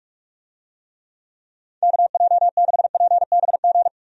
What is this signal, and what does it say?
Morse code, heard as a keyed tone, about 45 wpm, 700 Hz. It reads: KJ6PBG
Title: USA Call Signs - 6 Characters - 45wpm